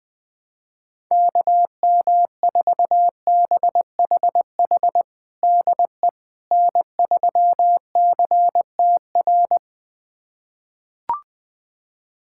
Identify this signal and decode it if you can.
Morse code, heard as a keyed tone, about 20 wpm, 700 Hz. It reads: KM4BHH DE N3CTR
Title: USA Callsign DE Callsign 20wpm